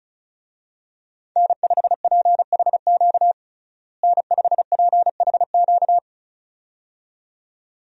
Morse code, heard as a keyed tone, about 36 wpm, 700 Hz. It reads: N5PHQ N5PHQ
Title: USA Call Signs - 5 Characters - Repeated 2x - 35wpm